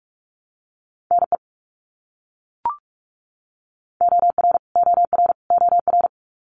Morse code, heard as a keyed tone, about 44 wpm, 700 Hz. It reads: NE OR OR OR
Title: US States Repeated 3x 45wpm